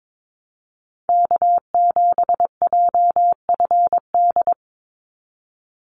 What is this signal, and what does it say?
Morse code, heard as a keyed tone, about 22 wpm, 700 Hz. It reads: K7JFD
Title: NA SOTA Activator Callsigns 22wpm